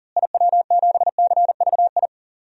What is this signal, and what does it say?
Morse code, heard as a keyed tone, about 39 wpm, 700 Hz. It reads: IW7CVI
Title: DXCC-100 Call Signs - 6 Characters - Repeated 3x - 40wpm